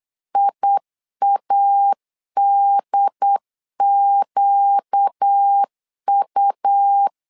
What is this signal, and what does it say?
Morse code, heard as a keyed tone, about 8 wpm, 800 Hz. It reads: IADQU
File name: LEZIONI / 4 Q1S 8wpm 2710